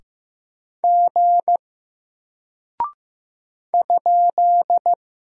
Morse code, heard as a keyed tone, about 15 wpm, 700 Hz. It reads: G ?
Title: Single Character (Letter:Number:Punctuation:Prosign) Alt 15wpm